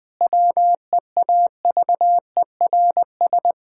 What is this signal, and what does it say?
Morse code, heard as a keyed tone, about 20 wpm, 700 Hz. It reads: WEAVERS
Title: Pangram Challenge v2 - 20wpm